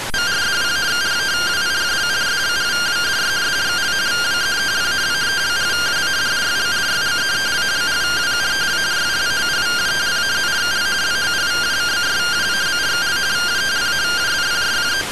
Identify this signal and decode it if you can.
Two-tone teleprinter signal, about 170 Hz shift.